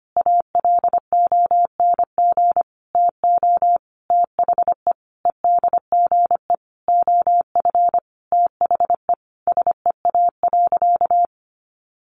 Morse code, heard as a keyed tone, about 25 wpm, 700 Hz. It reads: ALONG TO THE EDGE OF THE SEA.
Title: Sentences from Top 1300 Words 25wpm